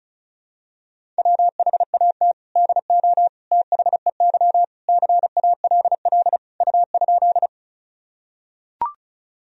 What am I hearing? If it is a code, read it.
Morse code, heard as a keyed tone, about 34 wpm, 700 Hz. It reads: WHAT DO THEY CALL U?
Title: Sentences from Top 100 Words and Abbreviations 35wpm